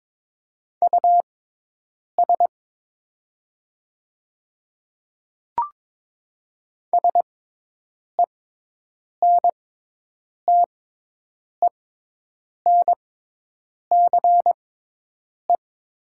Morse code, heard as a keyed tone, about 22 wpm, spaced as for 8 wpm, 700 Hz. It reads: US SENTENCE
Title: 6x Character Spacing - Top 200 Words - 22wpm